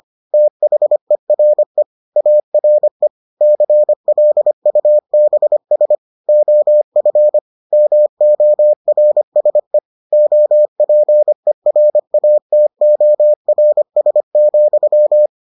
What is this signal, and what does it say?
Morse code, heard as a keyed tone, about 25 wpm, 600 Hz. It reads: THERE ARE CLUBS OF MORSE OPERATORS,